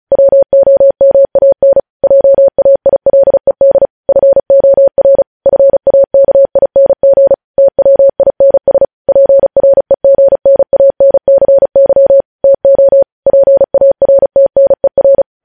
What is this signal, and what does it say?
Morse code, heard as a keyed tone, about 36 wpm, 550 Hz. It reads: WOMAN JAILED FOR FAKING TWINS PREGNANCY TO PARTNER
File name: podcast / News headlines-1x-35x35